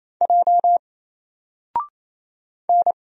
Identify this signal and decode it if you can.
Morse code, heard as a keyed tone, about 27 wpm, 700 Hz. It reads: J N
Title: Single Letter-Number Alt 28wpm